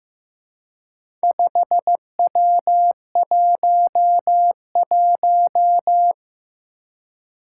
Morse code, heard as a keyed tone, about 15 wpm, 700 Hz. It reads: 5W11
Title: Four Challenging Letter-Number Combinations 15wpm